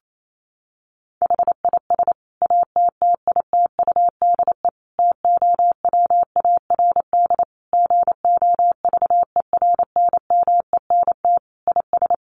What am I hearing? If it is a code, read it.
Morse code, heard as a keyed tone, about 28 wpm, 700 Hz. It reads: HIS ATTITUDE TOWARD GOVERNMENT IS
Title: Sentences from Top 1300 Words 28wpm